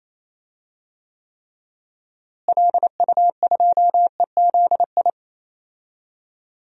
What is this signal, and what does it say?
Morse code, heard as a keyed tone, about 28 wpm, 700 Hz. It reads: LU2EZI
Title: DXCC-100 Call Signs - 6 Characters - Repeated 2x - 28wpm